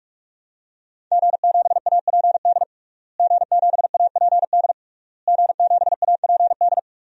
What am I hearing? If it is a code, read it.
Morse code, heard as a keyed tone, about 46 wpm, 700 Hz. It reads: G7APD G7APD G7APD
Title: DXCC-100 Call Signs - 5 Characters - Repeated 3x - 45wpm